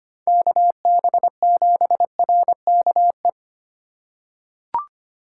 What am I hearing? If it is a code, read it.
Morse code, heard as a keyed tone, about 25 wpm, 700 Hz. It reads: KB7RKE
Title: USA Call Signs - 8x Character Spacing - 25wpm